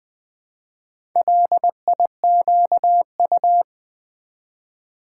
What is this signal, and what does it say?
Morse code, heard as a keyed tone, about 20 wpm, 700 Hz. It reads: LIQU
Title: Build A Word - 2 Letters At A Time - 6-Letter Words - 20wpm